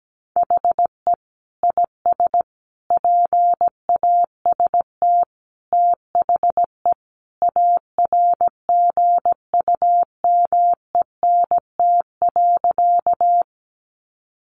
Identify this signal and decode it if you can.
Morse code, heard as a keyed tone, about 17 wpm, 700 Hz. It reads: HE IS PAST THE ARGUMENT.
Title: Sentences from Top 1400 Words 17wpm